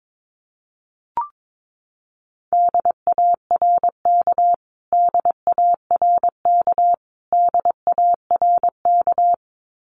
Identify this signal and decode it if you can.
Morse code, heard as a keyed tone, about 22 wpm, 700 Hz. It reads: DARK DARK DARK
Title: Top 500 Words Repeated 22wpm